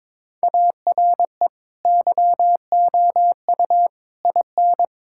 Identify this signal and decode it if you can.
Morse code, heard as a keyed tone, about 22 wpm, 700 Hz. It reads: ARE YOU IN
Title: Sentences from Top 1000 Words - Review Part 6 - 22wpm